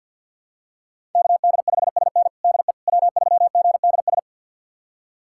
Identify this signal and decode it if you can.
Morse code, heard as a keyed tone, about 52 wpm, 700 Hz. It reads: KD5SN DE W3GDS
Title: USA Callsign DE Callsign 50wpm